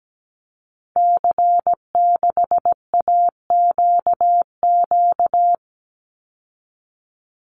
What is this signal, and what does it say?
Morse code, heard as a keyed tone, about 17 wpm, 700 Hz. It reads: C6AQQ
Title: Call Signs Repeated 2x - 17wpm